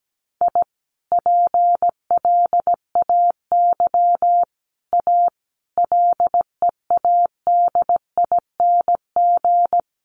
Morse code, heard as a keyed tone, about 17 wpm, 700 Hz. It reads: I PLAY A LEADING
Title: Sentences from Top 1100 Words 17wpm